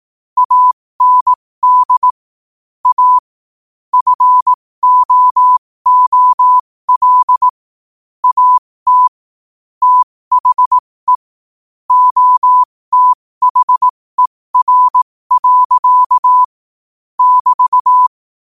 Morse code, heard as a keyed tone, about 18 wpm, spaced as for 15 wpm, 1000 Hz. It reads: AND A FOOL AT THE OTHER. <BT>
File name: qotdfiles / 15wpm